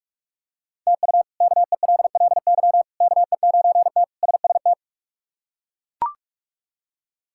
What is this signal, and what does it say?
Morse code, heard as a keyed tone, about 42 wpm, 700 Hz. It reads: TU KELLY KE9T SST
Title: K1USN SST QSO Practice - 45wpm